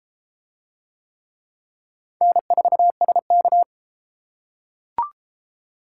Morse code, heard as a keyed tone, about 28 wpm, 700 Hz. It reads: N4SK